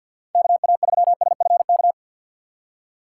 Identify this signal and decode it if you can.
Morse code, heard as a keyed tone, about 49 wpm, 690 Hz. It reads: KA3SFK
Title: USA Call Signs - 6 Characters - 3x Character Spacing - 50wpm